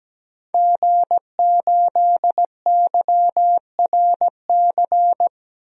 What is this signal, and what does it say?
Morse code, heard as a keyed tone, about 17 wpm, 700 Hz. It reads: G8YRC
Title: DXCC-100 Call Signs - 5 Characters - Repeated 2x - 17wpm